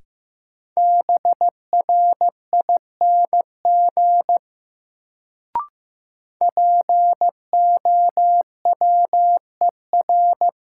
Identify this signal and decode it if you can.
Morse code, heard as a keyed tone, about 15 wpm, 700 Hz. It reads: BRING POWER
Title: Sentences from Top 500 Words - New Word Drill - 15wpm